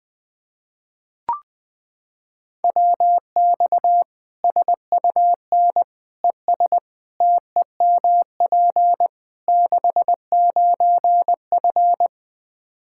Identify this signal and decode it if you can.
Morse code, heard as a keyed tone, about 20 wpm, 700 Hz. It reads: WX SUN ES TEMP 69F
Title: K1USN SST QSO Practice 20wpm